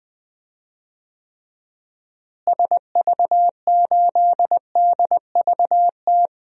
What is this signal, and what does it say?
Morse code, heard as a keyed tone, about 20 wpm, 700 Hz. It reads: SV8DVT
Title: DXCC-100 Call Signs - 6 Characters - Repeated 3x - 20wpm